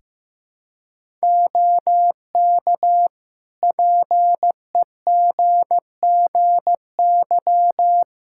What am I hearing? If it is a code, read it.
Morse code, heard as a keyed tone, about 15 wpm, 700 Hz. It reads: OK PEGGY
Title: Practice QSO - Exchange 3 - More Info - 15wpm